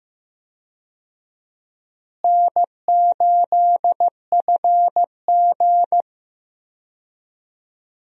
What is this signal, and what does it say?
Morse code, heard as a keyed tone, about 16 wpm, 700 Hz. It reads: N8FG
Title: USA Call Signs - 4 Characters - Repeated 2x - 15wpm